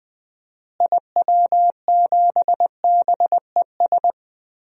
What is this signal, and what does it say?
Morse code, heard as a keyed tone, about 21 wpm, 700 Hz. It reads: IW7BES
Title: DXCC-100 Call Signs - 6 Characters - Repeated 3x - 20wpm